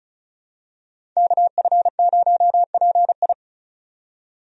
Morse code, heard as a keyed tone, about 36 wpm, 700 Hz. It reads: KF0PI
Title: USA Call Signs - 5 Characters - 35wpm